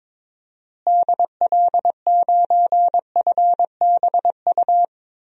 Morse code, heard as a keyed tone, about 21 wpm, 700 Hz. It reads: DL9FBU
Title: DXCC-100 Call Signs - 6 Characters - Repeated 3x - 22wpm